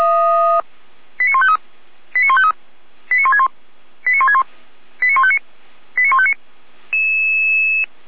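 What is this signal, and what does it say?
Five-tone selective call (ZVEI), heard as quick runs of five tones, tones of about 70 ms. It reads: 89153 89153 89161 89161 89159 89159
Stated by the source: ZVEI five-tone selective call received via SDR, fire brigade (Feuerwehr) Orlamünde.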